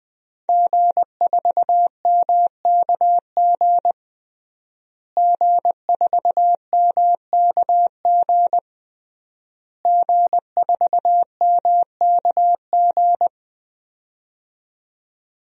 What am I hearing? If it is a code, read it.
Morse code, heard as a keyed tone, about 20 wpm, 700 Hz. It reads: G4MKG G4MKG G4MKG
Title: DXCC-100 Call Signs - 5 Characters - Repeated 3x - 20wpm